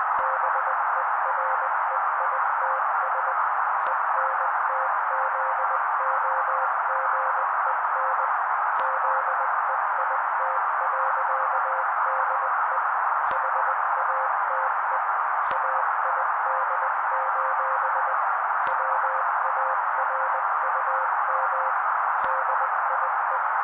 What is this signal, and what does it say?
Morse code, heard as a keyed tone, about 20 wpm, 530 Hz. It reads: BEREITS ENTZOGEN ZEIT.DE HATE AID: WARUM DIE
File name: daily / 18wpm noise